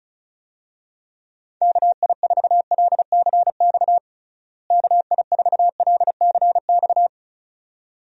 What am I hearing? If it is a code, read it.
Morse code, heard as a keyed tone, about 36 wpm, 700 Hz. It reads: KI4LCX KI4LCX
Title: USA Call Signs - 6 Characters - Repeated 2x - 35wpm